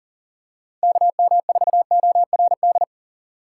Morse code, heard as a keyed tone, about 40 wpm, 700 Hz. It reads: KM4ORD
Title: USA Call Signs - 3x Character Spacing - 40wpm